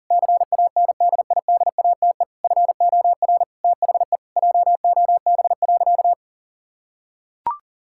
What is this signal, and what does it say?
Morse code, heard as a keyed tone, about 37 wpm, 700 Hz. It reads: CANDIDATE FOR THE JOB.
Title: Sentences from Top 1600 Words 40wpm